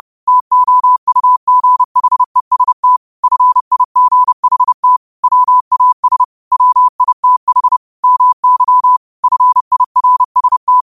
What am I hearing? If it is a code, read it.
Morse code, heard as a keyed tone, about 30 wpm, 1000 Hz. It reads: TOUGHEST FIGHT WAS WITH MY FIRST